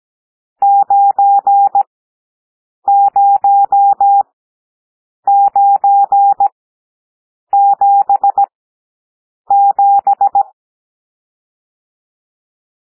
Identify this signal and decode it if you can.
Morse code, heard as a keyed tone, about 17 wpm, spaced as for 7 wpm, 800 Hz. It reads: 90977
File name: SOUND NUMBERS 17 wpm